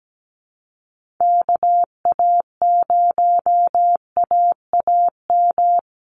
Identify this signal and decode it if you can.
Morse code, heard as a keyed tone, about 17 wpm, 700 Hz. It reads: KA0AAM
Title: Call Signs Repeated 2x - 17wpm